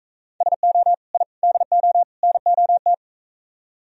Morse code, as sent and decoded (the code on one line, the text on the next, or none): ... --- / .. / -.. --- / -. --- -
SO I DO NOT